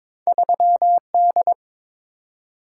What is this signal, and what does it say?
Morse code, heard as a keyed tone, about 22 wpm, 700 Hz. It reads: 3D
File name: practice / 22wpm / Two Letter-Number Combinations - 22wpm